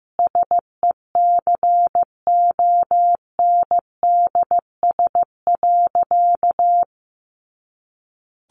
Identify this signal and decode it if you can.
Morse code, heard as a keyed tone, about 15 wpm, 700 Hz. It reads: SECONDS.
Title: A Sound Of Thunder 15wpm